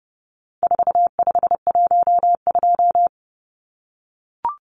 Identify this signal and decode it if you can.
Morse code, heard as a keyed tone, about 30 wpm, 700 Hz. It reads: 4512
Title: Numbers - 4-Digit 30wpm